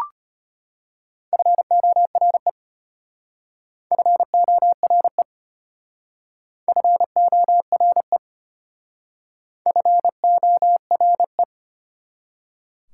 Morse code, heard as a keyed tone, about 32 wpm, 700 Hz. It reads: FORE FORE FORE FORE